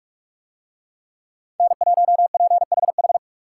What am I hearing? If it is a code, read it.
Morse code, heard as a keyed tone, about 46 wpm, 700 Hz. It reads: N1PHH